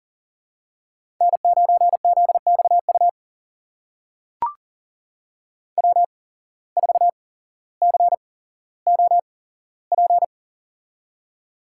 Morse code, heard as a keyed tone, about 40 wpm, spaced as for 17 wpm, 700 Hz. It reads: N9ZXU W4COP
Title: USA Call Signs - 5 Characters - 8x Character Spacing - 40wpm